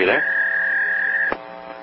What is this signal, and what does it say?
Fax machine handshake, with speech.
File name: hotel fax machine violation